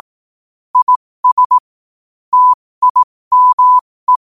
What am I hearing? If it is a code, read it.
Morse code, heard as a keyed tone, about 18 wpm, spaced as for 15 wpm, 1000 Hz. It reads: IS TIME
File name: qotdfiles / 15wpm